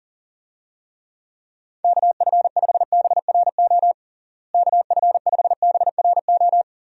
Morse code, heard as a keyed tone, about 40 wpm, 700 Hz. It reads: KF5BRO KF5BRO
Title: USA Call Signs - 6 Characters - Repeated 3x - 40wpm